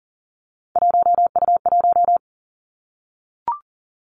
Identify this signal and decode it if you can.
Morse code, heard as a keyed tone, about 38 wpm, 690 Hz. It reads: 1U1
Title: Three Challenging Letter-Number Combinations 40wpm